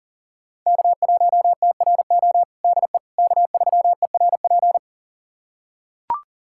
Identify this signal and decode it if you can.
Morse code, heard as a keyed tone, about 40 wpm, 700 Hz. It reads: K1TRO DE K3ERP
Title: USA Callsign DE Callsign 40wpm